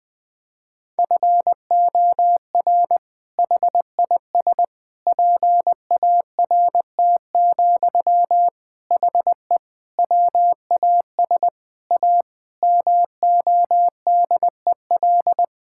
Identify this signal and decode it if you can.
Morse code, heard as a keyed tone, about 21 wpm, 700 Hz. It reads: FOR HIS PART, HE WAS A MODEL
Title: Sentences from Top 1000 Words - Review Part 5 - 20wpm